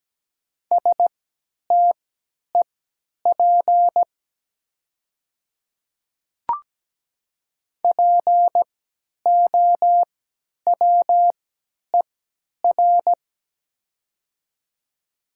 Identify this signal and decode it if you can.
Morse code, heard as a keyed tone, about 17 wpm, spaced as for 10 wpm, 700 Hz. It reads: STEP POWER
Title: 3x Character Spacing - Top 500 Words - 17wpm